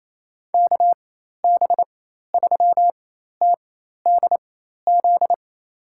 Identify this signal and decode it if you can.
Morse code, heard as a keyed tone, about 28 wpm, spaced as for 13 wpm, 700 Hz. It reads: KB3TDZ
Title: USA Call Signs - 6 Characters - 4x Character Spacing - 28wpm